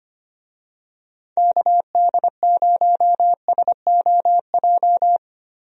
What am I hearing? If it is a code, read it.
Morse code, heard as a keyed tone, about 25 wpm, 700 Hz. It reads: KD0SOJ
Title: USA Call Signs - 6 Characters - Repeated 2x - 25wpm